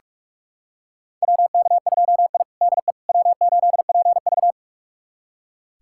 Morse code, heard as a keyed tone, about 46 wpm, 700 Hz. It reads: WK2I DE W8PV
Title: USA Callsign DE Callsign - 4 Characters - 45wpm